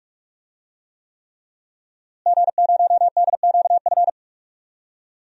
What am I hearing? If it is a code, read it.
Morse code, heard as a keyed tone, about 44 wpm, 700 Hz. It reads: G0DQF
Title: DXCC-100 Call Signs - 5 Characters - Repeated 3x - 45wpm